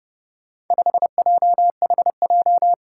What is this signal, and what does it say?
Morse code, heard as a keyed tone, about 30 wpm, 700 Hz. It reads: 5JHJ
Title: Four Challenging Letter-Number Combinations 30wpm